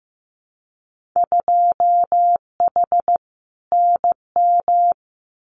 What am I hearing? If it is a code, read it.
Morse code, heard as a keyed tone, about 15 wpm, 700 Hz. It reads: 2H NM
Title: Winter Field Day Exchanges Encore 15wpm